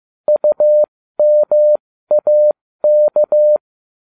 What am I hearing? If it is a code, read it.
Morse code, heard as a keyed tone, about 15 wpm, spaced as for 13 wpm, 600 Hz. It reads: UMAK